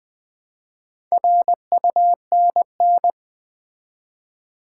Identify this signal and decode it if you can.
Morse code, heard as a keyed tone, about 20 wpm, 700 Hz. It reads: RUNN
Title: Build A Word - 2 Letters At A Time - 6-Letter Words - 20wpm